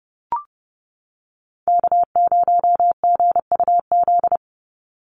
Morse code, heard as a keyed tone, about 30 wpm, 700 Hz. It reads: K0GUZ
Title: Call Signs - 30wpm